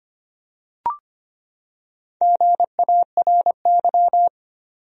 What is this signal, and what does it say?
Morse code, heard as a keyed tone, about 25 wpm, 700 Hz. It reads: GARY
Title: USA Common Names - Max 4 Letters - 25wpm